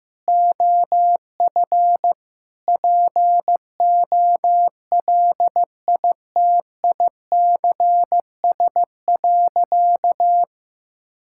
Morse code, heard as a keyed tone, about 16 wpm, 700 Hz. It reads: OF POLITICS.
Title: Sentences from Top 1000 Words - Review Part 6 - 15wpm